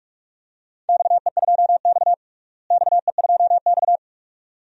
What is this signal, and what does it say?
Morse code, heard as a keyed tone, about 46 wpm, 700 Hz. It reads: XE2X XE2X
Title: DXCC-100 Call Signs - 4 Characters - Repeated 2x - 45wpm